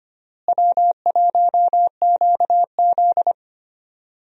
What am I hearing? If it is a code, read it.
Morse code, heard as a keyed tone, about 25 wpm, 700 Hz. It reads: W1QZ
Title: USA Call Signs - 4 Characters - Repeated 2x - 25wpm